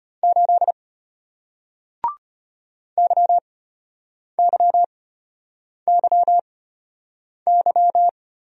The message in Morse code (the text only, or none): 8 YYYY